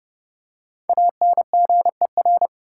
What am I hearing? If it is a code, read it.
Morse code, heard as a keyed tone, about 31 wpm, 700 Hz. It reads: ANGER
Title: Five Letter Words - 30wpm